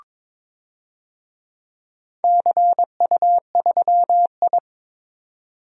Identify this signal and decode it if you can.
Morse code, heard as a keyed tone, about 22 wpm, 700 Hz. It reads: CU3I
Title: DXCC-100 Call Signs - 4 Characters - 22wpm